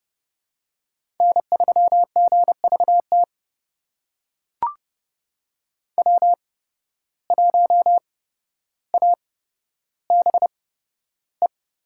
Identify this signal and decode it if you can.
Morse code, heard as a keyed tone, about 31 wpm, spaced as for 13 wpm, 700 Hz. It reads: N3GVT W1ABE
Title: USA Call Signs - 5 Characters - 8x Character Spacing - 30wpm